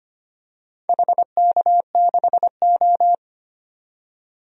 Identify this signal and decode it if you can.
Morse code, heard as a keyed tone, about 25 wpm, 700 Hz. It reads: HK6O
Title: DXCC-100 Call Signs - 4 Characters - 25wpm